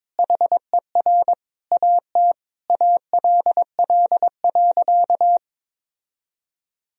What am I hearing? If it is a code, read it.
Morse code, heard as a keyed tone, about 22 wpm, 700 Hz. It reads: HER AT ALL.
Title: Sentences from Top 1000 Words - Review Part 2 - 22wpm